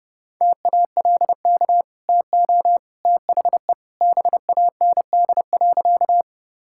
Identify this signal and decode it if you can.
Morse code, heard as a keyed tone, about 32 wpm, 700 Hz. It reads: TALK TO THE BAND.
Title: Sentences from Top 1600 Words 30wpm